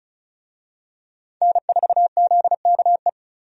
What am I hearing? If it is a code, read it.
Morse code, heard as a keyed tone, about 36 wpm, 700 Hz. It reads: N4ZKE
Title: USA Call Signs - 5 Characters - 35wpm